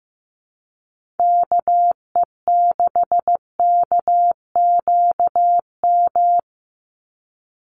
Morse code, heard as a keyed tone, about 15 wpm, 700 Hz. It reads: KE6KQM
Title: USA Call Signs 15wpm